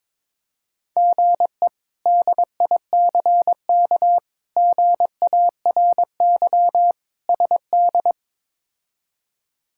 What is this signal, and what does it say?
Morse code, heard as a keyed tone, about 21 wpm, 700 Hz. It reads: GE DICK GARY SD
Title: K1USN SST QSO Practice - 22wpm